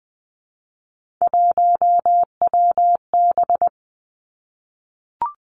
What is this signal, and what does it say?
Morse code, heard as a keyed tone, about 20 wpm, 700 Hz. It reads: 1WB
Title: Three Challenging Letter-Number Combinations 20wpm